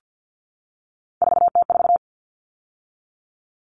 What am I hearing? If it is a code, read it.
Morse code, heard as a keyed tone, about 50 wpm, 700 Hz. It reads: 4T4
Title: Common Cut Numbers - 3-Digit 50wpm